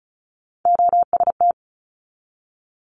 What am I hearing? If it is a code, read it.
Morse code, heard as a keyed tone, about 36 wpm, 700 Hz. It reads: OST
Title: Multi-letter Phonemes 35wpm